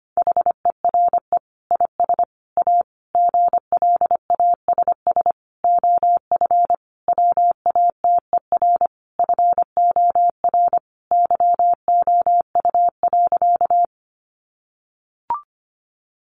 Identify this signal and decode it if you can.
Morse code, heard as a keyed tone, about 25 wpm, 700 Hz. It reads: HERE IS A GLASS OF WATER FOR YOU.
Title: Sentences from Top 1300 Words 25wpm